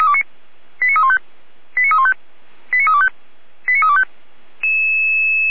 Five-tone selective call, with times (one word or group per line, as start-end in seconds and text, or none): 0.8-1.2 89315
1.8-2.1 89315
2.7-3.1 89325
3.7-4.0 89325